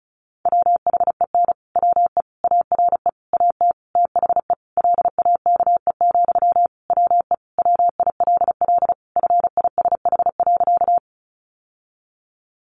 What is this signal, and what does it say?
Morse code, heard as a keyed tone, about 34 wpm, 700 Hz. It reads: WHEN WE ARE AT THE LAKE, WE WILL FISH.
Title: Sentences from Top 1100 Words 35wpm